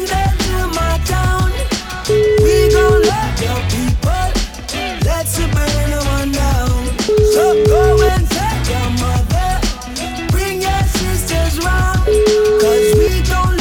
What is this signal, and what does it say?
Ringback tone (Europe, 425 Hz), with noise.